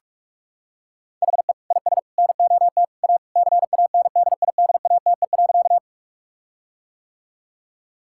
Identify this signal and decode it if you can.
Morse code, heard as a keyed tone, about 42 wpm, 700 Hz. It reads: HE IS NOT A CANDIDATE.
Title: Sentences from Top 1600 Words 45wpm